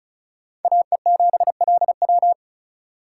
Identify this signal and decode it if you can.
Morse code, heard as a keyed tone, about 44 wpm, 700 Hz. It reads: AE7LW